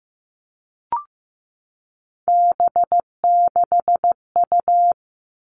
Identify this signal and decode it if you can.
Morse code, heard as a keyed tone, about 15 wpm, 700 Hz. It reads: B6U
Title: Three Challenging Letter-Number Combinations 15wpm